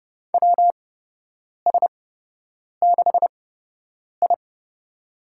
Morse code, heard as a keyed tone, about 30 wpm, spaced as for 8 wpm, 700 Hz. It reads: WS6I
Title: USA Call Signs - 4 Characters - 8x Character Spacing - 30wpm